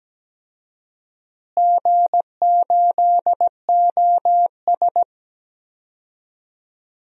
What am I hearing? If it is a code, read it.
Morse code, heard as a keyed tone, about 17 wpm, 700 Hz. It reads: G8OS
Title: DXCC-100 Call Signs - 4 Characters - Repeated 2x - 17wpm